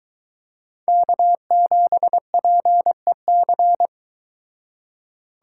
Morse code, heard as a keyed tone, about 22 wpm, 700 Hz. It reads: K7PEC